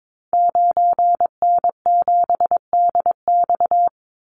Morse code, heard as a keyed tone, about 22 wpm, 700 Hz. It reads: 9N7DX
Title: Call Signs Repeated 3x - 22wpm